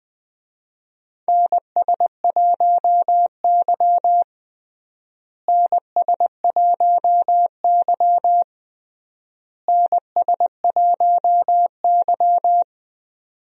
Morse code, heard as a keyed tone, about 20 wpm, 700 Hz. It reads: NS1Y NS1Y NS1Y
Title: USA Call Signs - 4 Characters - Repeated 3x - 20wpm